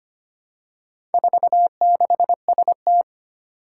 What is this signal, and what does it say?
Morse code, heard as a keyed tone, about 25 wpm, 700 Hz. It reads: 46ST
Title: Four Challenging Letter-Number Combinations 25wpm